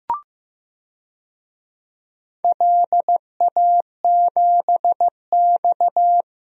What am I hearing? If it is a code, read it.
Morse code, heard as a keyed tone, about 15 wpm, 700 Hz. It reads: LA7X
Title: DXCC-100 Call Signs - 4 Characters - Repeated 2x - 15wpm